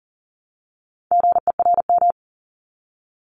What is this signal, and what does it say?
Morse code, heard as a keyed tone, about 41 wpm, 700 Hz. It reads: GERM
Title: Multi-letter Phonemes Repeated 3x 40wpm